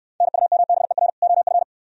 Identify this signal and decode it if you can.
Morse code, heard as a keyed tone, about 68 wpm, 700 Hz. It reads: NUMBER OF